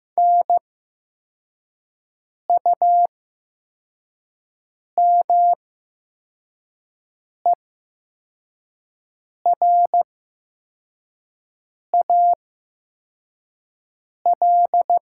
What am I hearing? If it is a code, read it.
Morse code, heard as a keyed tone, about 15 wpm, spaced as for 4 wpm, 700 Hz. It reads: NUMERAL
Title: 8x Character Spacing - Top 500 Words - 15wpm